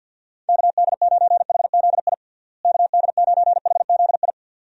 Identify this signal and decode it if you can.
Morse code, heard as a keyed tone, about 46 wpm, 700 Hz. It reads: KD9HZI KD9HZI
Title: USA Call Signs - 6 Characters - Repeated 2x - 50wpm